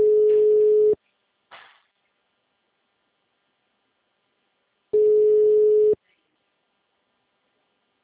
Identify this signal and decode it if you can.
Ringback tone (Europe, 425 Hz).
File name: out- -2010- - - .55988